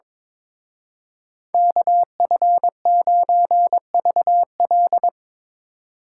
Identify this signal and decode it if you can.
Morse code, heard as a keyed tone, about 25 wpm, 700 Hz. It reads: KF9VL
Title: Speed Racing - USA 2x2 Call Signs 22wpm